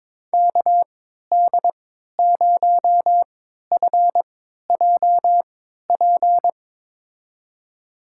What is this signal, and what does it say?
Morse code, heard as a keyed tone, about 22 wpm, spaced as for 13 wpm, 700 Hz. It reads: KD0FJP
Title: USA Call Signs - 6 Characters - 3x Character Spacing - 22wpm